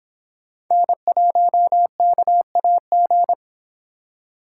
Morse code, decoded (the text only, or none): N1KAG